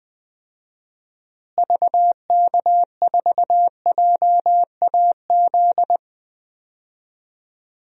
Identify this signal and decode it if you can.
Morse code, heard as a keyed tone, about 20 wpm, 700 Hz. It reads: VK4JAZ
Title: DXCC-100 Call Signs - 6 Characters - Repeated 3x - 20wpm